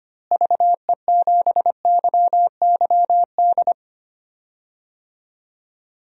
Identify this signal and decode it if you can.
Morse code, heard as a keyed tone, about 25 wpm, 700 Hz. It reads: VE7YYD
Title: DXCC-100 Call Signs - 6 Characters - Repeated 2x - 25wpm